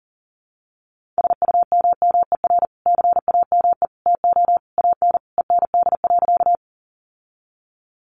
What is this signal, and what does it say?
Morse code, heard as a keyed tone, about 40 wpm, 700 Hz. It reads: SUMMER CAME TO AN END.
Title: Sentences from Top 1100 Words 40wpm